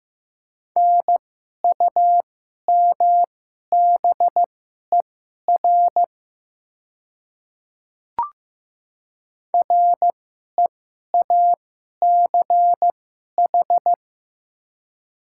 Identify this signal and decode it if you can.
Morse code, heard as a keyed tone, about 15 wpm, spaced as for 11 wpm, 700 Hz. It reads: NUMBER REACH
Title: 2x Character Spacing - Top 500 Words - 15wpm